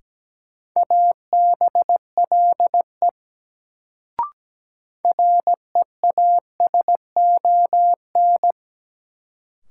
Morse code, heard as a keyed tone, about 17 wpm, 700 Hz. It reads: ABLE REASON
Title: Sentences from Top 700 Words - New Word Drill - 17wpm